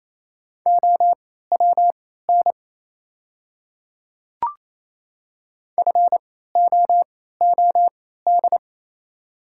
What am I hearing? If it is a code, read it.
Morse code, heard as a keyed tone, about 27 wpm, spaced as for 16 wpm, 690 Hz. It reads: OWN FOOD
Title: 3x Character Spacing - Top 500 Words - 28wpm